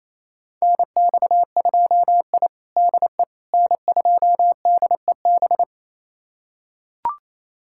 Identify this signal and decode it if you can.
Morse code, heard as a keyed tone, about 27 wpm, 700 Hz. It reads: NX2I DE N2DEB
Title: USA Callsign DE Callsign 28wpm